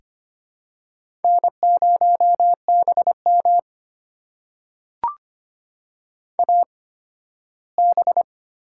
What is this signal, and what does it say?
Morse code, heard as a keyed tone, about 25 wpm, spaced as for 15 wpm, 700 Hz. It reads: N0BM AB
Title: USA Call Signs - 4 Characters - 8x Character Spacing - 25wpm